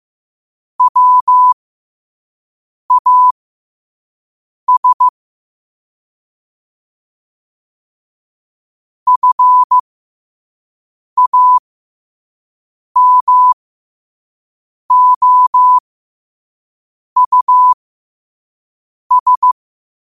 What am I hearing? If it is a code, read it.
Morse code, heard as a keyed tone, about 15 wpm, spaced as for 5 wpm, 1000 Hz. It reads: WAS FAMOUS
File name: qotdfiles / 5wpm